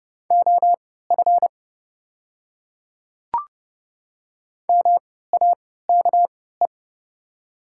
Morse code, heard as a keyed tone, about 30 wpm, spaced as for 17 wpm, 690 Hz. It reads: OF MAKE
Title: 3x Character Spacing - Top 500 Words - 30wpm